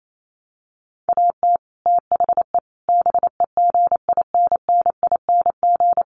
Morse code, heard as a keyed tone, about 28 wpm, 700 Hz. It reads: AT THE BEGINNING
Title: Sets of 4 Words 28wpm